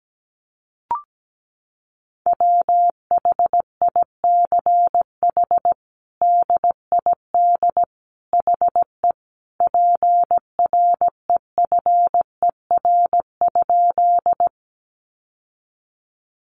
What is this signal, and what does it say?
Morse code, heard as a keyed tone, about 17 wpm, 700 Hz. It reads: WHICH DID HE PREFER?
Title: Sentences from Top 1100 Words 17wpm